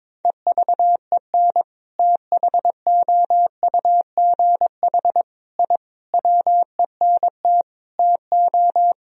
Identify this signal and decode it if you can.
Morse code, heard as a keyed tone, about 22 wpm, 700 Hz. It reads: EVEN THOUGH I WENT TO
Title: Sentences from Top 1000 Words - Review Part 2 - 22wpm